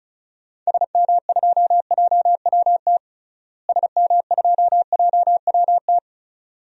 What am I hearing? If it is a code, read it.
Morse code, heard as a keyed tone, about 36 wpm, 700 Hz. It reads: SM2JWT SM2JWT
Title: DXCC-100 Call Signs - 6 Characters - Repeated 2x - 35wpm